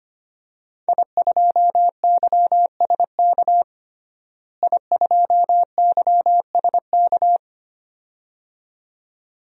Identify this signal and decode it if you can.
Morse code, heard as a keyed tone, about 25 wpm, 700 Hz. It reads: I2YSK I2YSK
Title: DXCC-100 Call Signs - 5 Characters - Repeated 2x - 25wpm